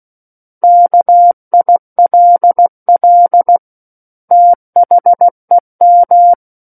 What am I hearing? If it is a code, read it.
Morse code, heard as a keyed tone, about 16 wpm, 700 Hz. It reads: KILL THEM
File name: Book One / 07-Chapter 7 - 16 WPM